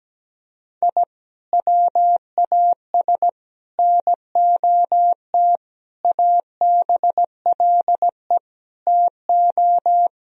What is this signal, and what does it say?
Morse code, heard as a keyed tone, about 17 wpm, 700 Hz. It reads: I WAS NOT ABLE TO
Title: Sentences from Top 1000 Words - Review Part 2 - 17wpm